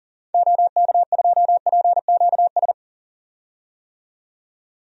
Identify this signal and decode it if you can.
Morse code, heard as a keyed tone, about 40 wpm, 700 Hz. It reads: OK2PQS
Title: DXCC-100 Call Signs - 6 Characters - 40wpm